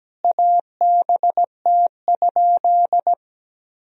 Morse code, heard as a keyed tone, about 17 wpm, 700 Hz. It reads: ABT?
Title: Sentences from Top 100 Words and Abbreviations 17wpm